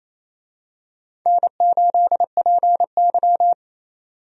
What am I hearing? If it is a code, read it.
Morse code, heard as a keyed tone, about 28 wpm, 700 Hz. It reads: N8PY
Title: USA Call Signs - 4 Characters - 28wpm